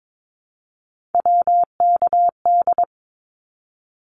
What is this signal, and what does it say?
Morse code, heard as a keyed tone, about 22 wpm, 700 Hz. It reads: WKD
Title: Abbreviations Repeated 3x - 22wpm